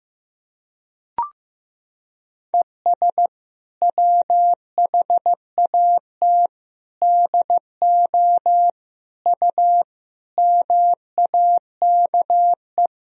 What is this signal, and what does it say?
Morse code, heard as a keyed tone, about 15 wpm, 700 Hz. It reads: ES WHAT DO U MAKE
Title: Sentences from Top 100 Words and Abbreviations 15wpm